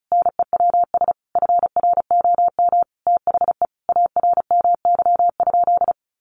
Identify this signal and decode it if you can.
Morse code, heard as a keyed tone, about 34 wpm, 700 Hz. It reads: NEWS FROM THE ARMY?
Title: Sentences from Top 800 Words 35wpm